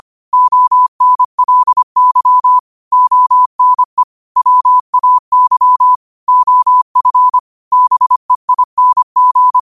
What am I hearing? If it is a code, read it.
Morse code, heard as a keyed tone, about 25 wpm, 1000 Hz. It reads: ONLY ONE WAY OF BEING
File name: qotdfiles / 25wpm